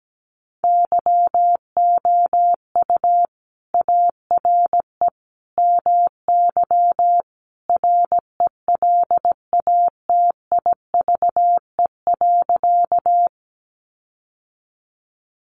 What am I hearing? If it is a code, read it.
Morse code, heard as a keyed tone, about 17 wpm, 700 Hz. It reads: YOU ARE MY RELATIVE.
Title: Sentences from Top 1400 Words 17wpm